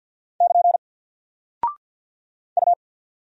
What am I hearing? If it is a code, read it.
Morse code, heard as a keyed tone, about 48 wpm, 700 Hz. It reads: <KN> U
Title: Single Character (Letter:Number:Punctuation:Prosign) Alt 50wpm